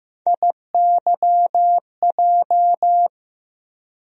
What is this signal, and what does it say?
Morse code, heard as a keyed tone, about 15 wpm, 700 Hz. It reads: IYJ
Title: USA Call Sign Postfixes - 3 Letters 15wpm